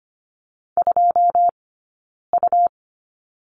Morse code, heard as a keyed tone, about 25 wpm, 700 Hz. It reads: 2 U
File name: Lesson 17d - Letter u - Two Characters 25wpm